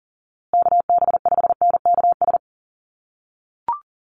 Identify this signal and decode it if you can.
Morse code, heard as a keyed tone, about 40 wpm, 700 Hz. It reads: KB5NKS
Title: USA Call Signs Repeated 3x 40wpm